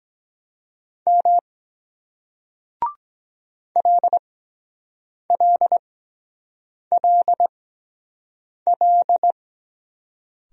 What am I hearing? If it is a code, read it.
Morse code, heard as a keyed tone, about 22 wpm, spaced as for 7 wpm, 700 Hz. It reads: M LLLL